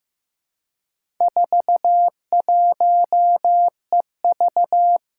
Morse code, heard as a keyed tone, about 15 wpm, 700 Hz. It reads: 41EV
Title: Four Challenging Letter-Number Combinations 15wpm